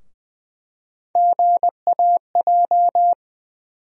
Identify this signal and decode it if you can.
Morse code, heard as a keyed tone, about 20 wpm, 700 Hz. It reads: GAJ
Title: USA Call Sign Postfixes - 3 Letters 20wpm